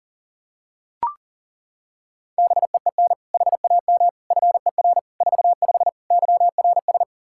Morse code, heard as a keyed tone, about 40 wpm, 700 Hz. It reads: BEEN HAM FER 45 YRS
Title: Practice QSO - Exchange 3 - More Info - 40wpm